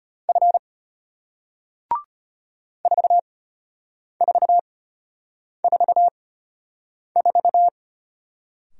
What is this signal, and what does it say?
Morse code, heard as a keyed tone, about 34 wpm, spaced as for 8 wpm, 700 Hz. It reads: F 4444